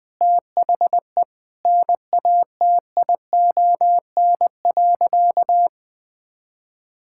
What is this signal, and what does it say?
Morse code, heard as a keyed tone, about 21 wpm, 700 Hz. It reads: THE NATION.
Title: Sentences with 4-Word Phrases - 20wpm